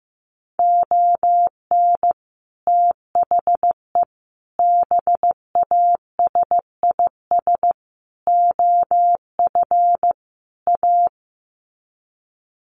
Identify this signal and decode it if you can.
Morse code, heard as a keyed tone, about 15 wpm, 700 Hz. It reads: ON THE BASIS OF A
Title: Sets of 5 Words Encore 15wpm